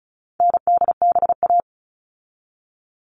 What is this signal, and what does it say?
Morse code, heard as a keyed tone, about 34 wpm, 700 Hz. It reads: NDBA
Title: All Cut Numbers - 4-Digit 35wpm